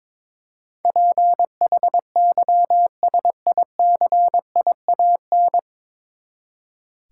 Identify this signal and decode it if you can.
Morse code, heard as a keyed tone, about 22 wpm, 700 Hz. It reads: PHYSICIAN
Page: Morse Code Ninja practice 22wpm